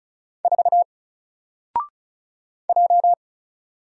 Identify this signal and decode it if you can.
Morse code, heard as a keyed tone, about 36 wpm, 700 Hz. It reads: 4 J
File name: Single-Letter-Number-35wpm